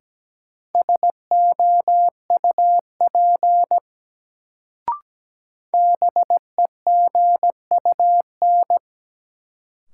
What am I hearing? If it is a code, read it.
Morse code, heard as a keyed tone, about 17 wpm, 700 Hz. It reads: SOUP BEGUN
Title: Sentences from Top 1000 Words - Review Part 6 - New Word Drill - 17wpm